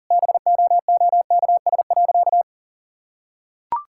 Morse code, heard as a keyed tone, about 37 wpm, 700 Hz. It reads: BOOKS.
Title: Sentences from Top 1000 Words - Review Part 6 - 40wpm